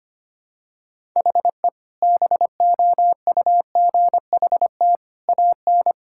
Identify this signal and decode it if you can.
Morse code, heard as a keyed tone, about 25 wpm, 700 Hz. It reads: HE BOUGHT AN
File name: practice / 25wpm / Sentences with 2-Word Phrases - 25wpm